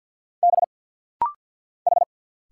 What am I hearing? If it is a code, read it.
Morse code, heard as a keyed tone, about 49 wpm, 700 Hz. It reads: B H
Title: Single Letter-Number Alt - Rapid-Fire 50wpm